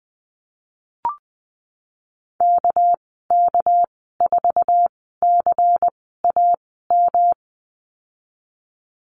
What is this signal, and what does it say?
Morse code, heard as a keyed tone, about 20 wpm, spaced as for 15 wpm, 700 Hz. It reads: KK4CAM
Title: 231 Lesson 37g - Letter x - Call Signs 20wpm